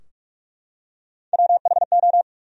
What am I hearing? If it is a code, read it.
Morse code, heard as a keyed tone, about 46 wpm, 700 Hz. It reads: WHO
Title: 4x Word-Spacing - Sentences from Top 300 Words - 45wpm